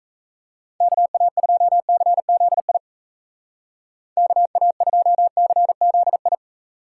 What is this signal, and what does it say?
Morse code, heard as a keyed tone, about 40 wpm, 700 Hz. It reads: KA2CZI KA2CZI